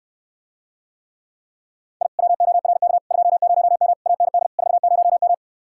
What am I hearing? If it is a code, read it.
Morse code, heard as a keyed tone, about 69 wpm, 700 Hz. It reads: I WORK 20M AND 40M